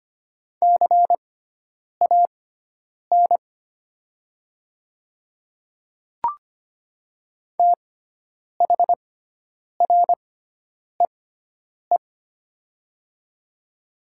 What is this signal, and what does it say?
Morse code, heard as a keyed tone, about 25 wpm, spaced as for 9 wpm, 700 Hz. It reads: CAN THREE
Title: 6x Character Spacing - Top 200 Words - 25wpm